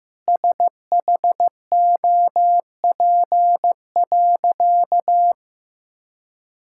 Morse code, heard as a keyed tone, about 15 wpm, 700 Hz. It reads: SHOP.
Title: Sentences from Top 1600 Words 15wpm